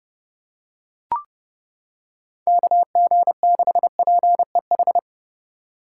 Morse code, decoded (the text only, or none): KG6PEH